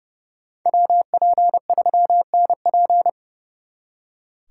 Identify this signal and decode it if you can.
Morse code, heard as a keyed tone, about 35 wpm, 700 Hz. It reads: WP3NP
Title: Speed Racing - USA 2x2 Call Signs 30wpm